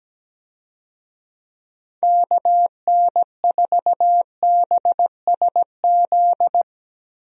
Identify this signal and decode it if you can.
Morse code, heard as a keyed tone, about 17 wpm, 700 Hz. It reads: KN4BSZ
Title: USA Call Signs - 6 Characters - 17wpm